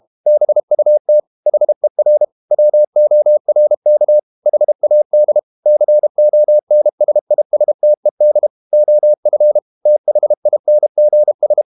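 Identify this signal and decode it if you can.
Morse code, heard as a keyed tone, about 32 wpm, 600 Hz. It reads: BUT HER WORK HAD CONSISTED OF THINGS
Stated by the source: Morse code practice at 32 wpm from CW med Gnister, reading Ken Follett's novel Never in English.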